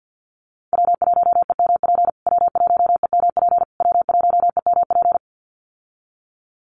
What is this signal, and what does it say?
Morse code, heard as a keyed tone, about 50 wpm, 700 Hz. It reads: W1EMP W1EMP W1EMP
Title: USA Call Signs Repeated 3x 50wpm